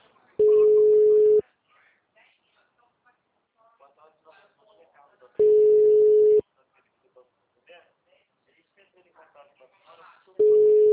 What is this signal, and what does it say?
Ringback tone (Europe, 425 Hz), with speech.
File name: out- -2032- - - .3530